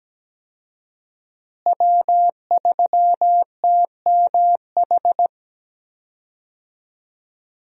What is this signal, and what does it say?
Morse code, heard as a keyed tone, about 17 wpm, 700 Hz. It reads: W3TMH
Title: USA Call Signs - 5 Characters - Repeated 2x - 17wpm